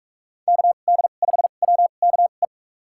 Morse code, heard as a keyed tone, about 46 wpm, spaced as for 33 wpm, 700 Hz. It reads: KD5WKE
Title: USA Call Signs - 2x Character Spacing - 45wpm